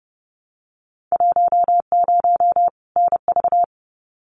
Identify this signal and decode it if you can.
Morse code, heard as a keyed tone, about 31 wpm, 700 Hz. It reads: 10 NV
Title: Winter Field Day Exchanges 30wpm